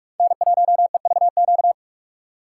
Morse code, heard as a keyed tone, about 44 wpm, 700 Hz. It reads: N1EVQ